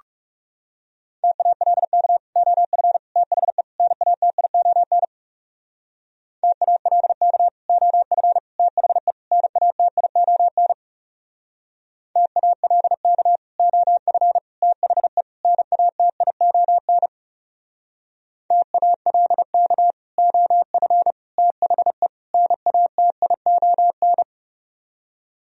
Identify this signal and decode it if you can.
Morse code, heard as a keyed tone, about 36 wpm, 700 Hz. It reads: TALK OF THE NATION TALK OF THE NATION TALK OF THE NATION TALK OF THE NATION
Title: Speed-Racing - Sets of 4 Words 30wpm